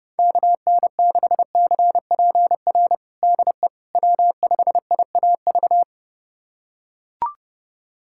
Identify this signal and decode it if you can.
Morse code, heard as a keyed tone, about 31 wpm, 700 Hz. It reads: KN6CPR DE W5IAV
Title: USA Callsign DE Callsign 30wpm